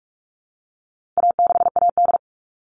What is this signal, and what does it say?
Morse code, heard as a keyed tone, about 44 wpm, 700 Hz. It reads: A6AD
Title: All Cut Numbers - 4-Digit 45wpm